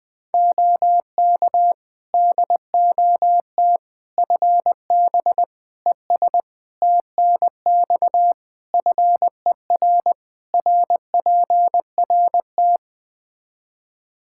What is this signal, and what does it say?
Morse code, heard as a keyed tone, about 20 wpm, 700 Hz. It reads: OK DOT FB ES TNX FER RPRT
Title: Practice QSO - Exchange 2 - Info - 20wpm